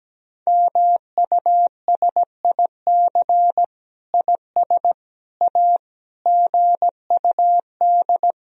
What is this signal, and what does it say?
Morse code, heard as a keyed tone, about 17 wpm, 700 Hz. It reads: MUSIC IS A GUD
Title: Sentences from Top 300 Words and Abbreviations 17wpm